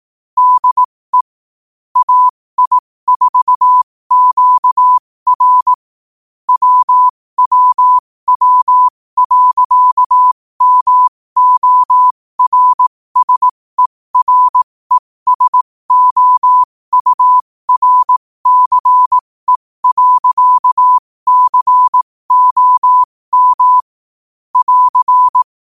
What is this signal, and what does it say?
Morse code, heard as a keyed tone, about 18 wpm, spaced as for 15 wpm, 1000 Hz. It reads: DE AI4QR WWW.MORSERESOURCE.COM <AR>
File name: qotdfiles / 15wpm